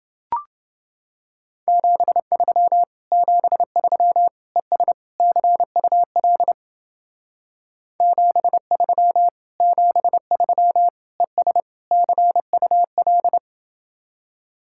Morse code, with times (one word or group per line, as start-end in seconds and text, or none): 1.7-2.8 73
3.1-4.3 73
4.6-4.9 ES
5.2-6.5 CUL
8.0-9.3 73
9.6-10.9 73
11.2-11.6 ES
11.9-13.4 CUL